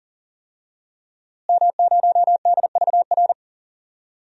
Morse code, heard as a keyed tone, about 37 wpm, 700 Hz. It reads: M0DVR
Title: DXCC-100 Call Signs - 5 Characters - Repeated 2x - 40wpm